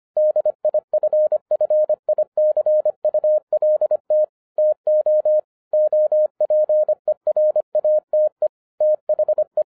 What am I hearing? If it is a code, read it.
Morse code, heard as a keyed tone, about 25 wpm, 610 Hz. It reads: DIFFICULT TO OPERATE THE
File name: 2012-01-mp3 / n08-ITS MY FAULT, BUT-25wpm